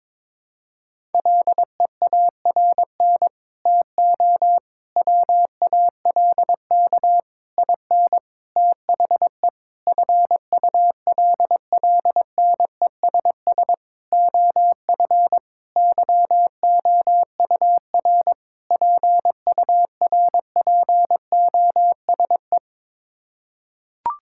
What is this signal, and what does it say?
Morse code, heard as a keyed tone, about 22 wpm, 700 Hz. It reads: LEARN TO WALK IN THE FULLNESS OF YOUR PURPOSE
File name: Sets of 9 Words - 22wpm